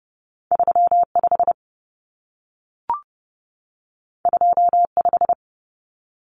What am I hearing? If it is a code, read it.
Morse code, heard as a keyed tone, about 30 wpm, 700 Hz. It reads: 35 25
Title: Numbers - 2-Digit 30wpm